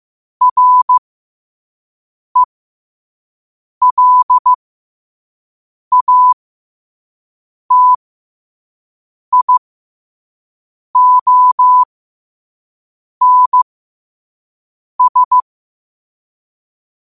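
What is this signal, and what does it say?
Morse code, heard as a keyed tone, about 15 wpm, spaced as for 5 wpm, 1000 Hz. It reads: RELATIONS
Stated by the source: Morse code sent at 5 wpm for learning.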